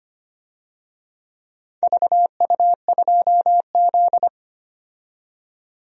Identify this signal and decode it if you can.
Morse code, heard as a keyed tone, about 25 wpm, 700 Hz. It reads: VU2Z
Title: DXCC-100 Call Signs - 4 Characters - 25wpm